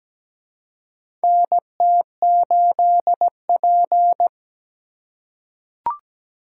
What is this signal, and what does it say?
Morse code, heard as a keyed tone, about 17 wpm, 700 Hz. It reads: NT8P
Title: USA Call Signs - 4x Character Spacing - 17wpm